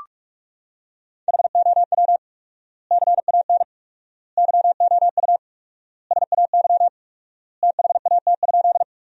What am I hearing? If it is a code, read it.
Morse code, heard as a keyed tone, about 42 wpm, 700 Hz. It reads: HOW CAN YOU SAY THAT?
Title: 4x Word-Spacing - Sentences from Top 200 Words - 45wpm